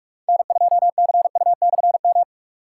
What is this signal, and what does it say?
Morse code, heard as a keyed tone, about 46 wpm, 700 Hz. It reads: N2CU/M
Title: K1USN SST QSO Practice - 45wpm